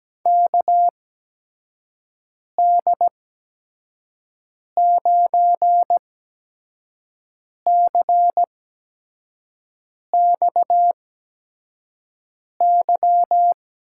Morse code, heard as a keyed tone, about 17 wpm, spaced as for 5 wpm, 700 Hz. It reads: KD9CXY
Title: USA Call Signs - 6 Characters - 8x Character Spacing - 17wpm